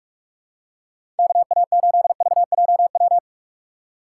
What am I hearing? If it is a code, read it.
Morse code, heard as a keyed tone, about 44 wpm, 700 Hz. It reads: KA8VJW
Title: USA Call Signs - 6 Characters - 2x Character Spacing - 45wpm